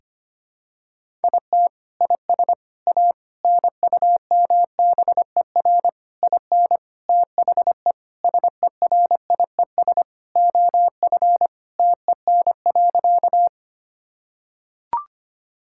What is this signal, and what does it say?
Morse code, heard as a keyed tone, about 25 wpm, 700 Hz. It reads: IT IS A NUMBER IN THE SERIES OF TEN.
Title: Sentences from Top 1000 Words - Review Part 5 - 25wpm